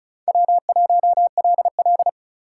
Morse code, heard as a keyed tone, about 34 wpm, 700 Hz. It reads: W1LL
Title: USA Call Signs - 4 Characters - 2x Character Spacing - 35wpm